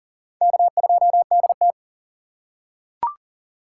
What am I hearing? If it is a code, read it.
Morse code, heard as a keyed tone, about 40 wpm, 700 Hz. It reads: K2DT
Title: USA Call Signs - 4 Characters - 6x Character Spacing - 40wpm